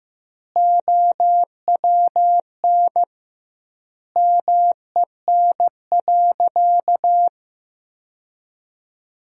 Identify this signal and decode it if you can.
Morse code, heard as a keyed tone, about 16 wpm, 700 Hz. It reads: OWN MEN.
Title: 2x Word-Spacing - Sentences from Top 300 Words - 15wpm